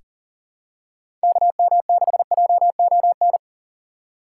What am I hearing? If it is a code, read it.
Morse code, heard as a keyed tone, about 40 wpm, 700 Hz. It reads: KM6JON
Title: USA Call Signs - 6 Characters - 2x Character Spacing - 40wpm